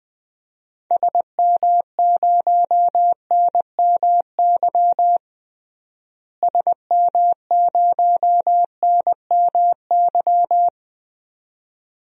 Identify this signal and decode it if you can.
Morse code, heard as a keyed tone, about 20 wpm, 700 Hz. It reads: SM0NMY SM0NMY
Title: DXCC-100 Call Signs - 6 Characters - Repeated 2x - 20wpm